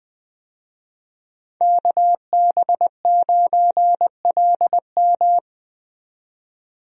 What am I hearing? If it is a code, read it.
Morse code, heard as a keyed tone, about 20 wpm, 700 Hz. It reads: KB9LM
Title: USA Call Signs - 5 Characters - 20wpm